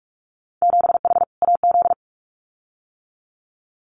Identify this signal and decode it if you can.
Morse code, heard as a keyed tone, about 46 wpm, 700 Hz. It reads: 7H AZ
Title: Winter Field Day Exchanges Encore 45wpm